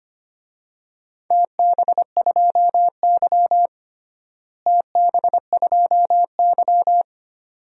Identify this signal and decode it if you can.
Morse code, heard as a keyed tone, about 25 wpm, 700 Hz. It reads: TB2Y TB2Y
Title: DXCC-100 Call Signs - 4 Characters - Repeated 2x - 25wpm